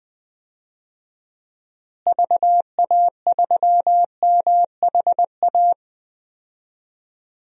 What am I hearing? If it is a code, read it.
Morse code, heard as a keyed tone, about 20 wpm, 700 Hz. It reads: VA3MHA
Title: DXCC-100 Call Signs - 6 Characters - Repeated 2x - 20wpm